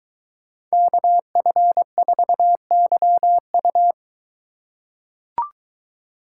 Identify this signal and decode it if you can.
Morse code, heard as a keyed tone, about 23 wpm, 700 Hz. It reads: KF4YU